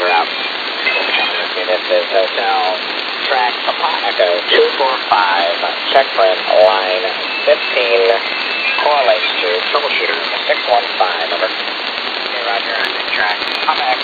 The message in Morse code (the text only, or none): none